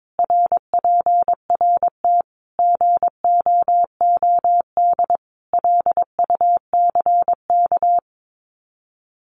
Morse code, heard as a keyed tone, about 22 wpm, 700 Hz. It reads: RPRT GOOD LUCK
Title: Common QSO Phrases 22wpm